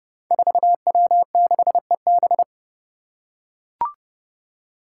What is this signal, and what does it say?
Morse code, heard as a keyed tone, about 30 wpm, 700 Hz. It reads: 4W6EB
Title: Five Challenging Letter-Number Combinations 30wpm